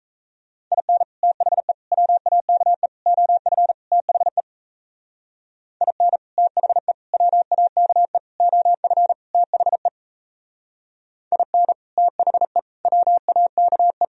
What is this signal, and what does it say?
Morse code, heard as a keyed tone, about 35 wpm, 700 Hz. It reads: IN THE WAKE OF THE IN THE WAKE OF THE IN THE WAKE